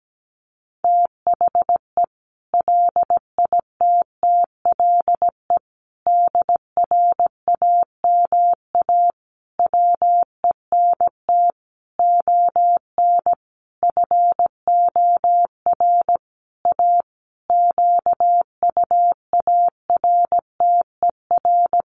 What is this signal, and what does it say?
Morse code, heard as a keyed tone, about 17 wpm, 700 Hz. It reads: THE LITTLE DRAMA WENT ON FOR A QUARTER
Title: Sentences from Top 1500 Words 17wpm